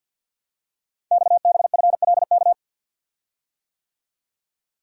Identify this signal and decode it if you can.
Morse code, heard as a keyed tone, about 52 wpm, 700 Hz. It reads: XBFLK
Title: Code Groups - 5 Letters - 50wpm